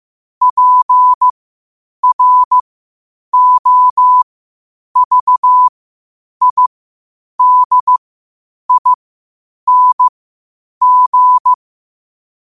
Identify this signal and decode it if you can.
Morse code, heard as a keyed tone, about 15 wpm, spaced as for 8 wpm, 1000 Hz. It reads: PROVIDING
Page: qotd-at-8-wpm